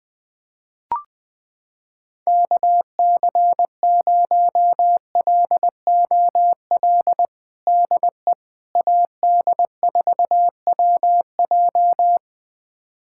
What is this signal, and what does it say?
Morse code, heard as a keyed tone, about 20 wpm, 700 Hz. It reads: KC0LOL DE AD4WJ
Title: USA Callsign DE Callsign 20wpm